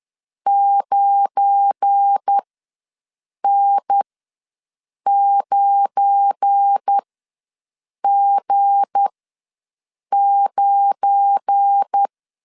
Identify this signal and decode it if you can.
Morse code, heard as a keyed tone, about 10 wpm, spaced as for 6 wpm, 800 Hz. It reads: 9N9G9
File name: LEZIONI / 7 GF9Z / 7 GF9 ws 10wpm 4034